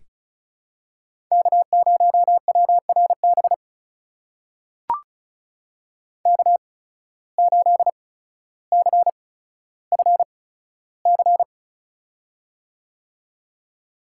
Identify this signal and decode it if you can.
Morse code, heard as a keyed tone, about 34 wpm, spaced as for 15 wpm, 700 Hz. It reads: K0WRB K8CFC
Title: USA Call Signs - 5 Characters - 8x Character Spacing - 35wpm